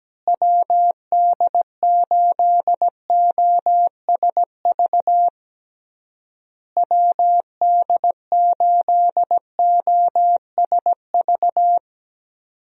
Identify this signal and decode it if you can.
Morse code, heard as a keyed tone, about 17 wpm, 700 Hz. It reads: WD8OSV WD8OSV
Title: USA Call Signs - 6 Characters - Repeated 3x - 17wpm